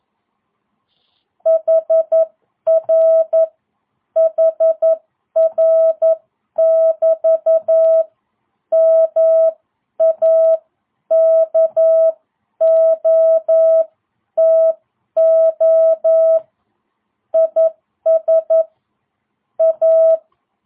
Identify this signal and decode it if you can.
Morse code, heard as a keyed tone, about 11 wpm, spaced as for 9 wpm, 660 Hz. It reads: HR HR<BT> MAKOTO IS A